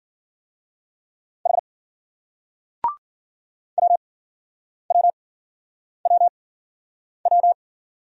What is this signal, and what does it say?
Morse code, heard as a keyed tone, about 56 wpm, spaced as for 9 wpm, 700 Hz. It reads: H WWWW